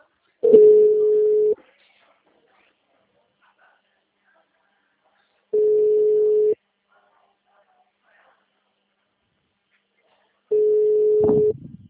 Ringback tone (Europe, 425 Hz).
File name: out- -2013- - - .0